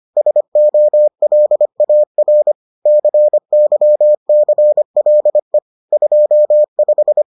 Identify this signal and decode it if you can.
Morse code, heard as a keyed tone, about 25 wpm, 600 Hz. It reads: SOLAR CYCLE 25